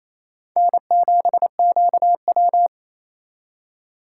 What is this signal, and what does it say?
Morse code, heard as a keyed tone, about 35 wpm, 700 Hz. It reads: N7QW